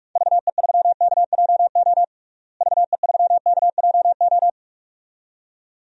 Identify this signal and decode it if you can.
Morse code, heard as a keyed tone, about 43 wpm, 700 Hz. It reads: VE3KJO VE3KJO
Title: DXCC-100 Call Signs - 6 Characters - Repeated 2x - 45wpm